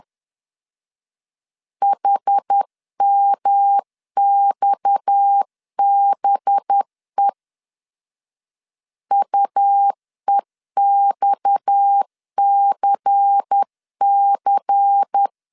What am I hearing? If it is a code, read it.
Morse code, heard as a keyed tone, about 10 wpm, 800 Hz. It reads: HMXBE UEXCC
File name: CW / DET / 1 6 1500C S C 10wpm 4252